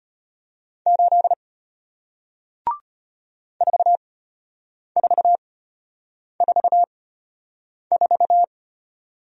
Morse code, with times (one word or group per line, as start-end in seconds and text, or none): 0.9-1.3 8
3.6-8.4 4444